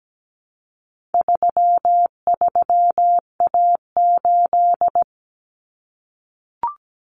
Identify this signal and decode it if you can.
Morse code, heard as a keyed tone, about 17 wpm, 700 Hz. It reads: 33A8
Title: Common Cut Numbers - 4-Digit 17wpm